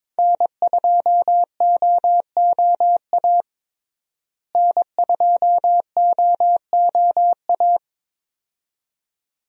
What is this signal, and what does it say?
Morse code, heard as a keyed tone, about 22 wpm, 700 Hz. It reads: N2OOA N2OOA
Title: USA Call Signs - 5 Characters - Repeated 2x - 22wpm